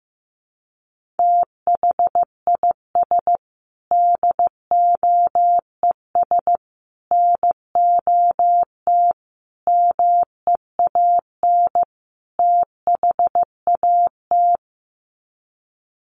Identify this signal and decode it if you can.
Morse code, heard as a keyed tone, about 15 wpm, 700 Hz. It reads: THIS DOES NOT MEAN THAT
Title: Sets of 5 Words Encore2 15wpm